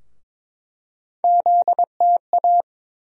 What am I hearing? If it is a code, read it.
Morse code, heard as a keyed tone, about 22 wpm, 700 Hz. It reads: ZTA
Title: USA Call Sign Postfixes - 3 Letters 22wpm